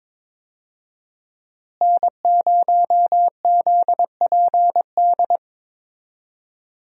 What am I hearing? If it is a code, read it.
Morse code, heard as a keyed tone, about 22 wpm, 700 Hz. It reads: N0ZPD
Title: USA Call Signs - 5 Characters - Repeated 3x - 22wpm